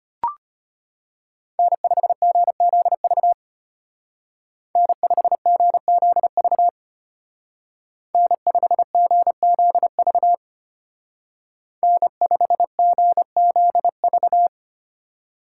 Morse code, heard as a keyed tone, about 31 wpm, 700 Hz. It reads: N5GZV N5GZV N5GZV N5GZV
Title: Speed Racing - USA Call Signs 25wpm